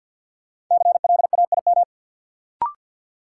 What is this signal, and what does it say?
Morse code, heard as a keyed tone, about 53 wpm, 700 Hz. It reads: CLAIM
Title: Sentences from Top 1500 Words - New Word Drill - 50wpm